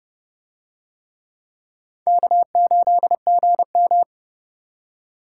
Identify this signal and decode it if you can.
Morse code, heard as a keyed tone, about 29 wpm, 700 Hz. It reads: K8GM